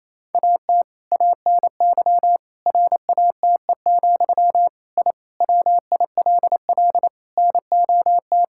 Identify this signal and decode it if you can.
Morse code, heard as a keyed tone, about 27 wpm, 700 Hz. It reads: AT ANY RATE, I WILL NOT
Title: Sentences from Top 1000 Words - Review Part 6 - 28wpm